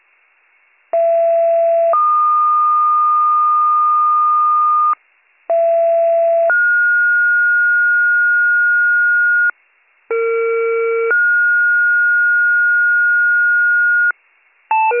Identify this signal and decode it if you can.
Two-tone page, heard as a one-second tone, then a three-second tone, about 670 then 1220 Hz, about 670 then 1430 Hz, about 460 then 1430 Hz, with noise. It is heard from 0.9 to 14.1 s.